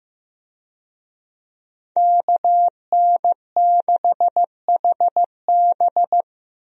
Morse code, heard as a keyed tone, about 15 wpm, 700 Hz. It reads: KN6HB
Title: USA Call Signs - 5 Characters - Repeated 3x - 15wpm